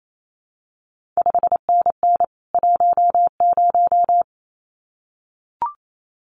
Morse code, heard as a keyed tone, about 28 wpm, 700 Hz. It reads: 5NN 10
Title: CQ WW DX Contest Exchange Encore - 28wpm